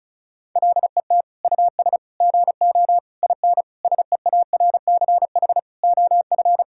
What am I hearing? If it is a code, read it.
Morse code, heard as a keyed tone, about 36 wpm, 700 Hz. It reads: LET US GO IN SEARCH OF
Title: Sentences from Top 1000 Words - Review Part 5 - 35wpm